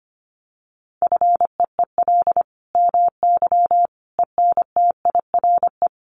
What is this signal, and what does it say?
Morse code, heard as a keyed tone, about 25 wpm, 700 Hz. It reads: FEEL MY ENTIRE
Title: Sentences from Top 800 Words 25wpm